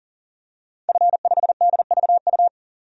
Morse code, heard as a keyed tone, about 40 wpm, 700 Hz. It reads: F5DVU
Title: DXCC-100 Call Signs - 5 Characters - 40wpm